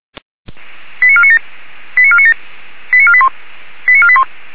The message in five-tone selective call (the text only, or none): 89497 89497 89481 89481